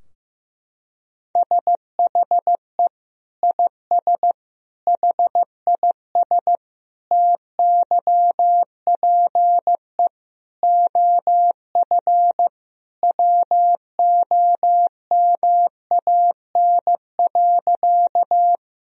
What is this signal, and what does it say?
Morse code, heard as a keyed tone, about 15 wpm, 700 Hz. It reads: SHE IS HIS TYPE OF WOMAN.
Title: Sentences from Top 1000 Words - Review Part 3 - 15wpm